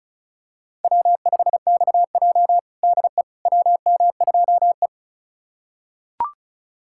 Morse code, heard as a keyed tone, about 34 wpm, 700 Hz. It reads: W5XJ DE WM2E
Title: USA Callsign DE Callsign - 4 Characters - 35wpm